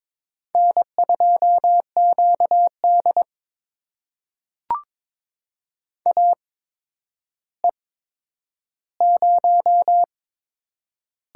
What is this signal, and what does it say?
Morse code, heard as a keyed tone, about 22 wpm, spaced as for 11 wpm, 700 Hz. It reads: N2QD AE0
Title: USA Call Signs - 4 Characters - 8x Character Spacing - 22wpm